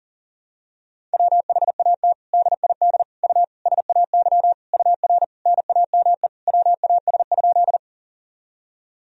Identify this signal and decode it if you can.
Morse code, heard as a keyed tone, about 40 wpm, 700 Hz. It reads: WHAT DID U SAY UR NAME WAS?
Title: Sentences from Top 200 Words and Abbreviations 40wpm